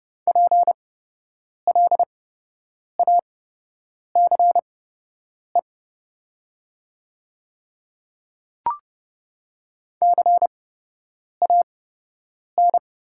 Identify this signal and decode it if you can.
Morse code, heard as a keyed tone, about 30 wpm, spaced as for 8 wpm, 700 Hz. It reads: PLACE CAN
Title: 8x Character Spacing - Top 200 Words - 30wpm